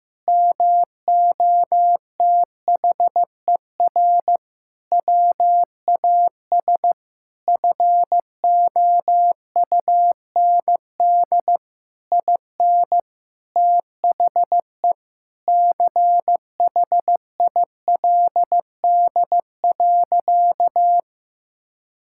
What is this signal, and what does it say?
Morse code, heard as a keyed tone, about 15 wpm, 700 Hz. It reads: MOTHER WAS FOUND IN THE CHILD.
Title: Sentences from Top 1000 Words - Review Part 5 - 15wpm